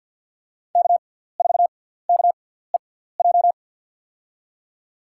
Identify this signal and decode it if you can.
Morse code, heard as a keyed tone, about 48 wpm, spaced as for 17 wpm, 700 Hz. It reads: K4KEJ
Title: USA Call Signs - 6x Character Spacing - 50wpm